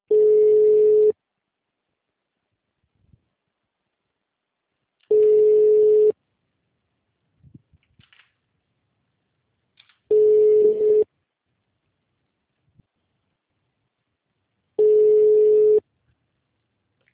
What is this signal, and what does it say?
Ringback tone (Europe, 425 Hz).
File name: out- -2031- - - .737